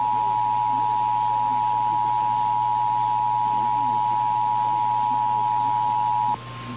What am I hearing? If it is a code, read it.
Emergency Alert System (SAME) tones (the 853 and 960 Hz attention tone). It is heard from 0.9 to 6.0 s.
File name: eas / KNCYKBIE / KNCYKBIE